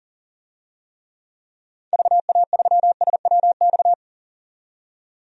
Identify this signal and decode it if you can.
Morse code, heard as a keyed tone, about 40 wpm, 700 Hz. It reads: VA3SWX
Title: DXCC-100 Call Signs - 6 Characters - 40wpm